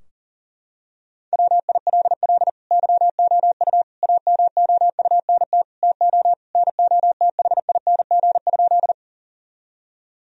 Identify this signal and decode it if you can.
Morse code, heard as a keyed tone, about 40 wpm, 700 Hz. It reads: WILL YOU AMOUNT TO NOTHING?
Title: Sentences from Top 1000 Words - Review Part 4 - 40wpm